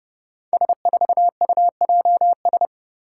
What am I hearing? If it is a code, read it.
Morse code, heard as a keyed tone, about 32 wpm, 700 Hz. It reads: S4UJS
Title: Five Challenging Letter-Number Combinations 30wpm